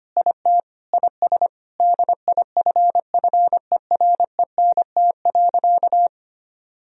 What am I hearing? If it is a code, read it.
Morse code, heard as a keyed tone, about 25 wpm, 700 Hz. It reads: IT IS DIFFERENT.
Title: Sentences from Top 1000 Words - Review Part 3 - 25wpm